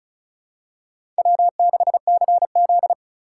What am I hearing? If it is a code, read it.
Morse code, heard as a keyed tone, about 36 wpm, 700 Hz. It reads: W6CZ
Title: USA Call Signs - 4 Characters - 3x Character Spacing - 35wpm